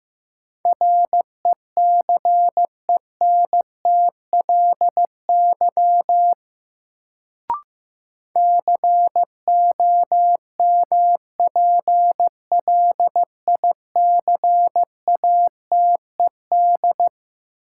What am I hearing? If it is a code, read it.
Morse code, heard as a keyed tone, about 15 wpm, 700 Hz. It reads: RECENTLY COMPLICATED
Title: Sentences from Top 1200 Words - New Word Drill Alt - 15wpm